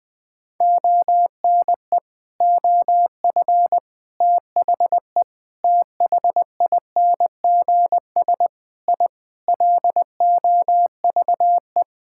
Morse code, heard as a keyed tone, about 20 wpm, 700 Hz. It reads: ONE OF THE THINGS I LOVE
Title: Sentences with 4-Word Phrases - Spoken Beginning - 20wpm